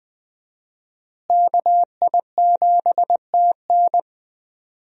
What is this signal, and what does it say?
Morse code, heard as a keyed tone, about 21 wpm, 700 Hz. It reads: KI7TN
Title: USA Call Signs - 5 Characters - Repeated 2x - 20wpm